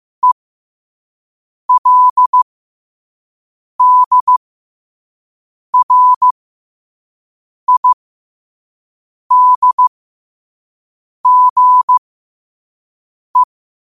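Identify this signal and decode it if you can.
Morse code, heard as a keyed tone, about 15 wpm, spaced as for 5 wpm, 1000 Hz. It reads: ELDRIDGE
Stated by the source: Morse code at 5 words per minute.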